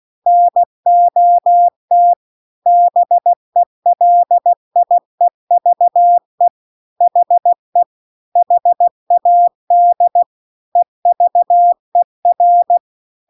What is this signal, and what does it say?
Morse code, heard as a keyed tone, about 16 wpm, 700 Hz. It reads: NOT BELIEVE HE HAD EVER